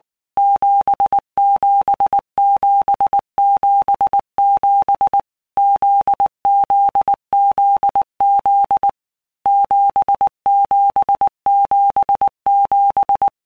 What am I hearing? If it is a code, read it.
Morse code, heard as a keyed tone, about 19 wpm, 800 Hz. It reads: 77777 ZZZZ 7777
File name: All Groups 1-minute Blocks / Groups@18 / JZ17